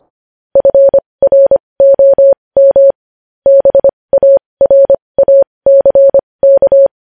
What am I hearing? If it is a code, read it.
Morse code, heard as a keyed tone, about 25 wpm, spaced as for 20 wpm, 550 Hz. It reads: FROM BARACK
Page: News Headlines in Morse Code at 20 WPM